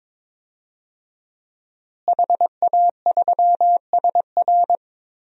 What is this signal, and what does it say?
Morse code, heard as a keyed tone, about 22 wpm, 700 Hz. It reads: HA3SR
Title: DXCC-100 Call Signs - 5 Characters - Repeated 3x - 22wpm